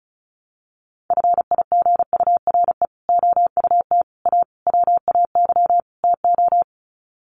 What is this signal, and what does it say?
Morse code, heard as a keyed tone, about 34 wpm, 700 Hz. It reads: FIGURE OUT A WAY TO
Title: Sets of 5 Words Encore2 35wpm